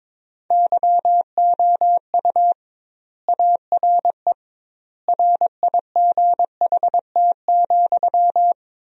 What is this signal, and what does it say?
Morse code, heard as a keyed tone, about 22 wpm, 700 Hz. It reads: YOU ARE RIGHT,
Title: 2x Word-Spacing - Sentences from Top 400 Words - 22wpm